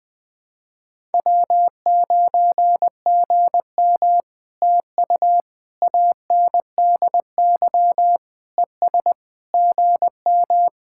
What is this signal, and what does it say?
Morse code, heard as a keyed tone, about 20 wpm, 700 Hz. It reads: W9GM TU ANDY ES GM
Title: POTA QSO Practice - 20wpm